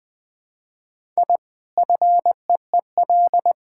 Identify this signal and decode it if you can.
Morse code, heard as a keyed tone, about 21 wpm, 700 Hz. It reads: I FEEL
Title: Sentences from Top 1000 Words - Review Part 6 - 20wpm